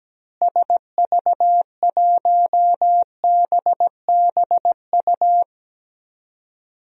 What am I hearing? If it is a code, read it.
Morse code, heard as a keyed tone, about 17 wpm, 700 Hz. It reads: SV1BBU
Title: DXCC-100 Call Signs - 6 Characters - Repeated 2x - 17wpm